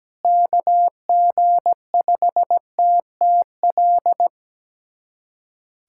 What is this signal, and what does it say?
Morse code, heard as a keyed tone, about 17 wpm, 700 Hz. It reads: KG5TTL
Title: USA Callsign DE Callsign 17wpm